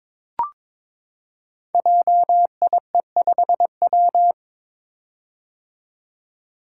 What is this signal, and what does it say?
Morse code, heard as a keyed tone, about 22 wpm, 690 Hz. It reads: JIE5W
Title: Code Groups - 5 Letter:Numbers - 22wpm